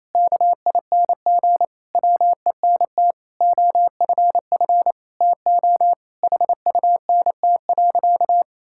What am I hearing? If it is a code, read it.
Morse code, heard as a keyed tone, about 28 wpm, 700 Hz. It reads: KING WENT OFF TO HUNT.
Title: Sentences from Top 1600 Words 28wpm